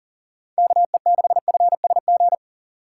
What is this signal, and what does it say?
Morse code, heard as a keyed tone, about 40 wpm, 700 Hz. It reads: KE6FSG
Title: USA Call Signs - 6 Characters - 2x Character Spacing - 40wpm